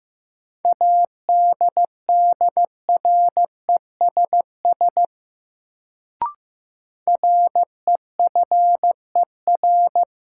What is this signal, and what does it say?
Morse code, heard as a keyed tone, about 15 wpm, 700 Hz. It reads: ADDRESS REFER
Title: Sentences from Top 1000 Words - New Word Drill - 15wpm